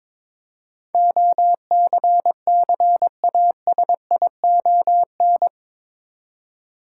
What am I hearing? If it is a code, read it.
Morse code, heard as a keyed tone, about 22 wpm, 700 Hz. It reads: OCCASION
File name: practice / 22wpm / Eight Letter Words - 22wpm